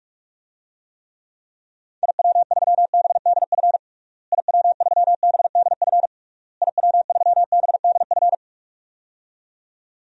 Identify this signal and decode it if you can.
Morse code, heard as a keyed tone, about 46 wpm, 700 Hz. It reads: IW3BDF IW3BDF IW3BDF
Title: DXCC-100 Call Signs - 6 Characters - Repeated 3x - 45wpm